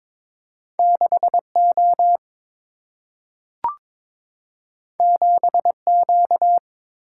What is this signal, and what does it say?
Morse code, heard as a keyed tone, about 22 wpm, 700 Hz. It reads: 6 O 7 Q
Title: Single-Letter Single-Number 22wpm